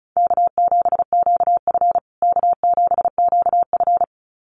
Morse code, heard as a keyed tone, about 34 wpm, 700 Hz. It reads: K7QF K7QF
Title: USA Call Signs Repeated 2x 35wpm